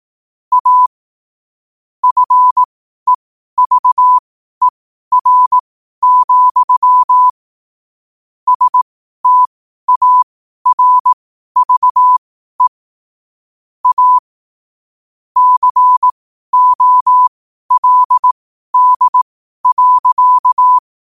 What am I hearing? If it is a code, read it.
Morse code, heard as a keyed tone, about 18 wpm, spaced as for 12 wpm, 1000 Hz. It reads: A FEVER, STARVE A COLD.
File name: qotdfiles / 12wpm